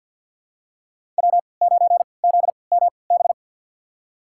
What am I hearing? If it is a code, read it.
Morse code, heard as a keyed tone, about 49 wpm, spaced as for 28 wpm, 700 Hz. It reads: W9ZMB
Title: USA Call Signs - 5 Characters - 3x Character Spacing - 50wpm